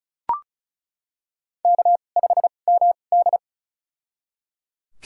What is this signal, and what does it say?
Morse code, heard as a keyed tone, about 34 wpm, spaced as for 25 wpm, 700 Hz. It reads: K5MD
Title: USA Call Signs - 4 Characters - 2x Character Spacing - 35wpm